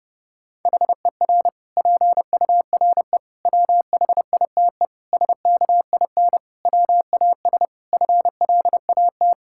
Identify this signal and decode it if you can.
Morse code, heard as a keyed tone, about 30 wpm, 700 Hz. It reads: HER PURE WHITE SKIN WAS FLAT
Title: Sentences from Top 1600 Words 30wpm